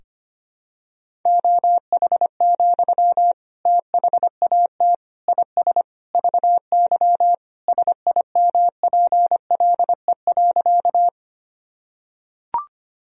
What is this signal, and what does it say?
Morse code, heard as a keyed tone, about 25 wpm, 700 Hz. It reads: OH, THAT IS VY SIMPLE.
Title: Sentences from Top 500 Words and Abbreviations 25wpm